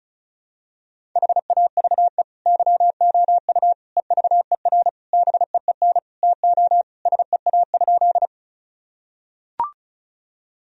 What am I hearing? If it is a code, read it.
Morse code, heard as a keyed tone, about 36 wpm, 700 Hz. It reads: HAVE YOU EVER BEEN TO SEA?
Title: Sentences from Top 1000 Words - Review Part 1 - 35wpm